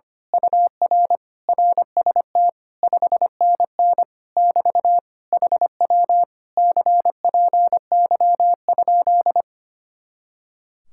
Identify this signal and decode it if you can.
Morse code, heard as a keyed tone, about 24 wpm, 700 Hz. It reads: UR RST 5NN <BT> HW CPY?
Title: Speed-Racing Common QSO Phrases Encore - 25wpm